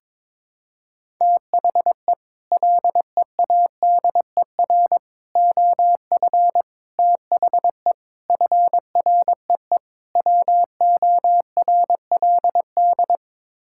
Morse code, heard as a keyed tone, about 22 wpm, 700 Hz. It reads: THE LEADER OF THE FREE WORLD
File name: practice / 22wpm / Sets of 6 Words - 22wpm